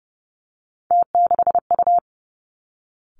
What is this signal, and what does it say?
Morse code, heard as a keyed tone, about 30 wpm, 700 Hz. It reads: T6U
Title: Three Challenging Letter-Number Combinations 30wpm